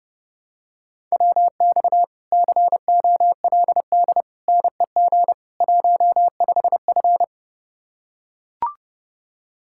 Morse code, heard as a keyed tone, about 30 wpm, 700 Hz. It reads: WX COLD NEG 15F